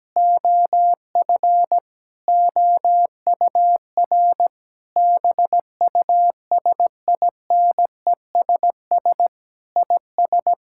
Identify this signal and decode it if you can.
Morse code, heard as a keyed tone, about 17 wpm, 700 Hz. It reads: OF OUR BUSINESS IS
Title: Sentences from Top 1000 Words - Review Part 6 - 17wpm